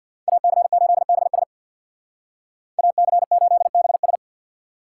Morse code, heard as a keyed tone, about 60 wpm, 700 Hz. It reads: AC8BS AC8BS
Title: Speed Racing - USA Call Signs 50wpm